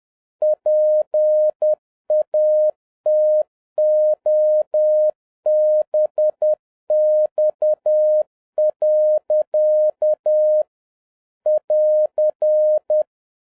Morse code, heard as a keyed tone, about 10 wpm, 610 Hz. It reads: PATOBX. <AR>